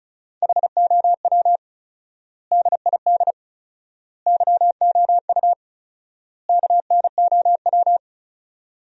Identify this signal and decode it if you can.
Morse code, heard as a keyed tone, about 36 wpm, 700 Hz. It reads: HOW DID YOU KNOW
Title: 4x Word-Spacing - Sentences from Top 100 Words - 35wpm